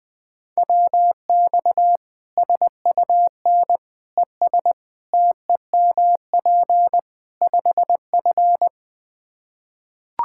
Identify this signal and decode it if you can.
Morse code, heard as a keyed tone, about 19 wpm, 700 Hz. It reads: WX SUN ES TEMP 5F
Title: Practice QSO - Exchange 2 - Info - 20wpm